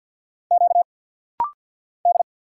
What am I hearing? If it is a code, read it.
Morse code, heard as a keyed tone, about 50 wpm, 690 Hz. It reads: Q D
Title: Single Letters - Rapid-Fire 50wpm